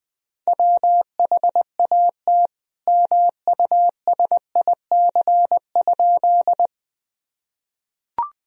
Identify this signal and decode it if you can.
Morse code, heard as a keyed tone, about 21 wpm, 700 Hz. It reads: WHAT MUSIC?
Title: Sentences from Top 1000 Words - Review Part 1 - 20wpm